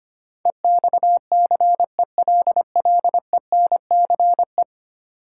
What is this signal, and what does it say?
Morse code, heard as a keyed tone, about 25 wpm, 700 Hz. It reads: EXCELLENCE
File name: Ten Letter Words - 25wpm